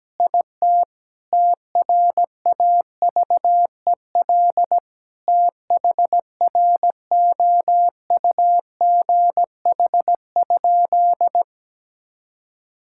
Morse code, heard as a keyed tone, about 17 wpm, 700 Hz. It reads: IT TRAVEL THROUGH?
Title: Sentences from Top 1000 Words - Review Part 6 - 17wpm